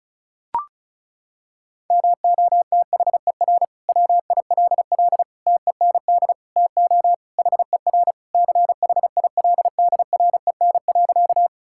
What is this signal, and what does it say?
Morse code, heard as a keyed tone, about 34 wpm, 700 Hz. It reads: MOTHER WILL TEND TO HER CHILDREN.
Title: Sentences from Top 1000 Words - Review Part 5 - 35wpm